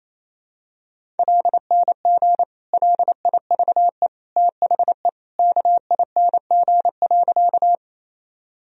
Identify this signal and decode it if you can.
Morse code, heard as a keyed tone, about 27 wpm, 700 Hz. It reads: LNG LIVE THE KING.
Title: Sentences from Top 400 Words and Abbreviations 28wpm